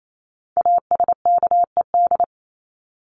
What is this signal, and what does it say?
Morse code, heard as a keyed tone, about 28 wpm, 700 Hz. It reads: ASKED
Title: ED Suffix - Top 100 Words with Base Words 28wpm